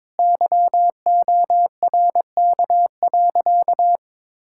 Morse code, heard as a keyed tone, about 22 wpm, 700 Hz. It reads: YORK.
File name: practice / 22wpm / Sentences with 3-Word Phrases - 22wpm